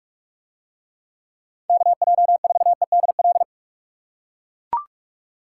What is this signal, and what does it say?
Morse code, heard as a keyed tone, about 40 wpm, 700 Hz. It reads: KJ4EDL